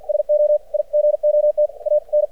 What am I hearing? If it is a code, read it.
Morse code, heard as a keyed tone, about 50 wpm, spaced as for 44 wpm, 600 Hz. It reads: SO I GOT AN
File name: archived / CRSnet